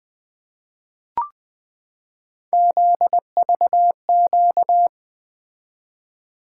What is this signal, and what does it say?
Morse code, heard as a keyed tone, about 20 wpm, 700 Hz. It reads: ZVQ
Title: Three Letter-Number Combinations - 20wpm